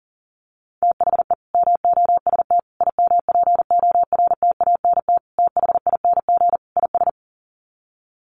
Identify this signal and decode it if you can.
Morse code, heard as a keyed tone, about 40 wpm, 700 Hz. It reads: THE MOST IMPORTANT THING IS
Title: Sets of 5 Words Encore 40wpm